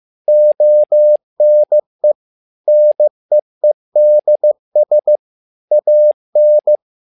Morse code, heard as a keyed tone, about 15 wpm, 600 Hz. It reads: ONE NEEDS AN
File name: learning center / tower trouble / tower trouble 15wpm